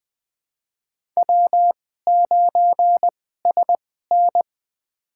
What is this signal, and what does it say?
Morse code, heard as a keyed tone, about 20 wpm, spaced as for 14 wpm, 700 Hz. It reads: W9SN
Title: USA Call Signs - 4 Characters - 2x Character Spacing - 20wpm